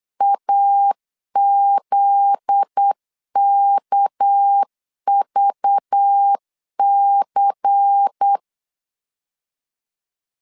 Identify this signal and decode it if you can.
Morse code, heard as a keyed tone, about 8 wpm, 800 Hz. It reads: AZKVC
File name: CW / DET / 1 9 1500C S C 8wpm 5653